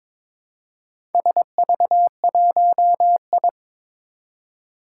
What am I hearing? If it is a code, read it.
Morse code, heard as a keyed tone, about 23 wpm, 700 Hz. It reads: SV1I
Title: Four Challenging Letter-Number Combinations 22wpm